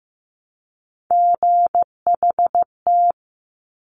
Morse code, heard as a keyed tone, about 15 wpm, 700 Hz. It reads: GHT
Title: Top Three Letter Combinations in Words 15wpm